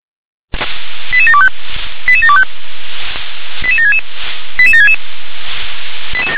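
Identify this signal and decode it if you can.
Five-tone selective call (ZVEI), heard as quick runs of five tones, tones of about 70 ms. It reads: 88725 88725 88766 88766
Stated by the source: ZVEI five-tone selective call 88150 received by SDR.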